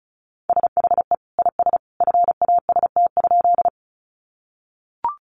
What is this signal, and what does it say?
Morse code, heard as a keyed tone, about 34 wpm, 700 Hz. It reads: SHE IS FAST?
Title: Sentences from Top 400 Words 35wpm